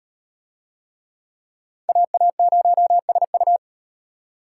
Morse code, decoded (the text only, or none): AA0SU